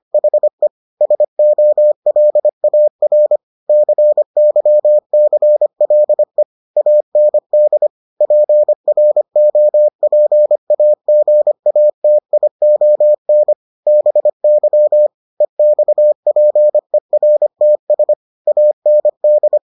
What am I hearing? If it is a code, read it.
Morse code, heard as a keyed tone, about 25 wpm, 600 Hz. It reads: HE SOLAR CYCLE AND PROPAGATION BY EXPERTS AND